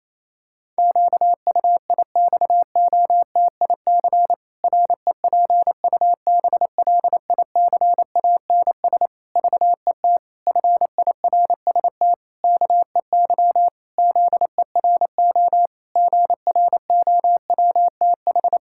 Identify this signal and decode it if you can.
Morse code, heard as a keyed tone, about 28 wpm, 700 Hz. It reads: QUIXOTIC REPUBLICANS VET FIRST KEY ZERO GROWTH
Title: Pangram Challenge v2 - 28wpm